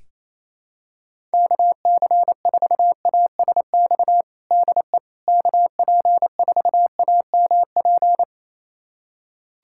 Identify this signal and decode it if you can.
Morse code, heard as a keyed tone, about 28 wpm, 700 Hz. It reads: KC4ASX DE KP4AMP
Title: USA Callsign DE Callsign 28wpm